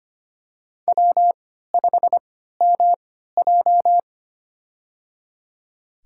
Morse code, heard as a keyed tone, about 25 wpm, spaced as for 14 wpm, 700 Hz. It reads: W5MJ
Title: USA Call Signs - 4 Characters - 3x Character Spacing - 25wpm